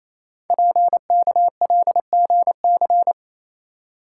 Morse code, heard as a keyed tone, about 29 wpm, 700 Hz. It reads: PKLGC
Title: Code Groups - 5 Letter:Numbers - 28wpm